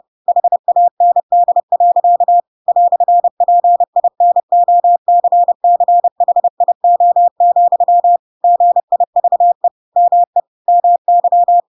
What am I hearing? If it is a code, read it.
Morse code, heard as a keyed tone, about 30 wpm, 700 Hz. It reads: HAND. "PINOCCHIO, GIVE ME MY
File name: Pinocchio / Pinocchio ch3 30 30 ik1hge